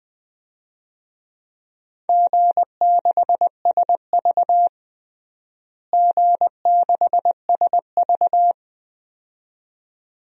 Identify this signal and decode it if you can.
Morse code, heard as a keyed tone, about 20 wpm, 700 Hz. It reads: G6SV G6SV
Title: DXCC-100 Call Signs - 4 Characters - Repeated 2x - 20wpm